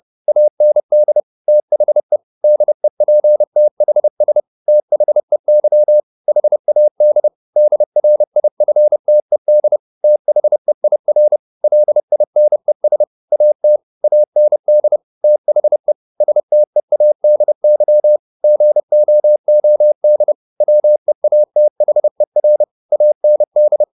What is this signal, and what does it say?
Morse code, heard as a keyed tone, about 30 wpm, 600 Hz. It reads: AND THE DEPTHS THEY HAD DRIFTED THEIR LINES AT AND THE STEADY GOOD WEATHER AND
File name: the old man and the sea / the old man and the sea 1 30wpm